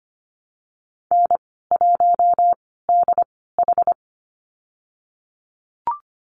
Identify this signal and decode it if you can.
Morse code, heard as a keyed tone, about 25 wpm, spaced as for 16 wpm, 700 Hz. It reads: N1DH
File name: Lesson 15g - Number 5 - Call Signs 25wpm